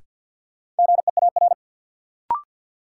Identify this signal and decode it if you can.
Morse code, heard as a keyed tone, about 53 wpm, 700 Hz. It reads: GEAR
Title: Sentences from Top 1400 Words - New Word Drill - 50wpm